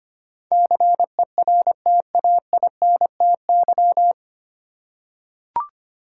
Morse code, heard as a keyed tone, about 25 wpm, 700 Hz. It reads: CERTAINTY
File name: practice / 25wpm / 9 Letter Words - First 3 Letters Given - 25wpm